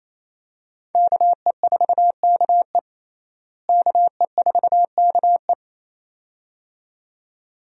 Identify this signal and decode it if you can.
Morse code, heard as a keyed tone, about 28 wpm, 700 Hz. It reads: KE4KE KE4KE
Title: USA Call Signs - 5 Characters - Repeated 2x - 28wpm